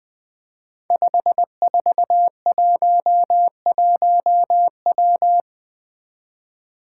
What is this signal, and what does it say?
Morse code, heard as a keyed tone, about 20 wpm, 700 Hz. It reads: 5411W
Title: Five Challenging Letter-Number Combinations 20wpm